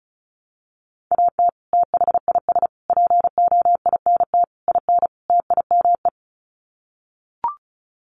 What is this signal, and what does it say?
Morse code, heard as a keyed tone, about 34 wpm, 700 Hz. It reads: AT THIS POINT IN TIME
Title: Sets of 5 Words Encore 35wpm